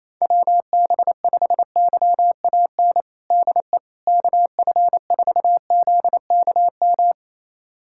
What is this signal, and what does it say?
Morse code, heard as a keyed tone, about 28 wpm, 700 Hz. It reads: WB5YAN DE KF4ZKM
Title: USA Callsign DE Callsign 28wpm